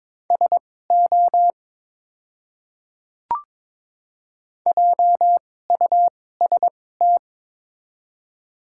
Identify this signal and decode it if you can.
Morse code, heard as a keyed tone, about 22 wpm, spaced as for 16 wpm, 700 Hz. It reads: SO JUST
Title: 2x Character Spacing - Top 500 Words - 22wpm